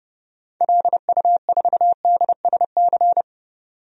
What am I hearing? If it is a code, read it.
Morse code, heard as a keyed tone, about 30 wpm, 700 Hz. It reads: LU4DSC
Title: DXCC-100 Call Signs - 6 Characters - 30wpm